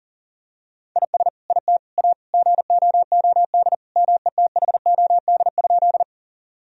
Morse code, heard as a keyed tone, about 40 wpm, 700 Hz. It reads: IS IT A GOOD METHOD?
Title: Sentences from Top 1000 Words - Review Part 3 - 40wpm